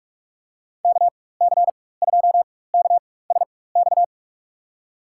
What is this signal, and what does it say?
Morse code, heard as a keyed tone, about 47 wpm, spaced as for 21 wpm, 700 Hz. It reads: KC2KSX
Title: USA Call Signs - 6 Characters - 4x Character Spacing - 45wpm